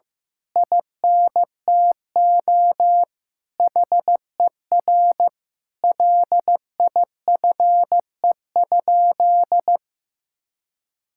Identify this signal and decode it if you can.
Morse code, heard as a keyed tone, about 15 wpm, 700 Hz. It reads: INTO HER LIFE?
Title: Sentences from Top 1000 Words - Review Part 6 - 15wpm